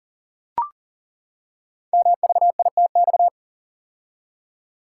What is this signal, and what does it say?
Morse code, heard as a keyed tone, about 42 wpm, 690 Hz. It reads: MVITX
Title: Code Groups - 5 Letters - 40wpm